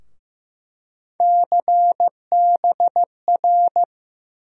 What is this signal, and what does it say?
Morse code, heard as a keyed tone, about 15 wpm, 700 Hz. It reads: CBR
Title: USA Call Sign Postfixes - 3 Letters 15wpm